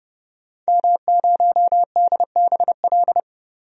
Morse code, heard as a keyed tone, about 30 wpm, 700 Hz. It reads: M0DBL
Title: DXCC-100 Call Signs - 5 Characters - 30wpm